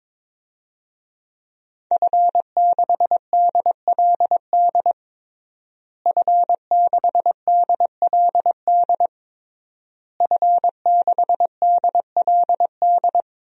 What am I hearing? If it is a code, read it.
Morse code, heard as a keyed tone, about 22 wpm, 700 Hz. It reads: F6DLD F6DLD F6DLD
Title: DXCC-100 Call Signs - 5 Characters - Repeated 3x - 22wpm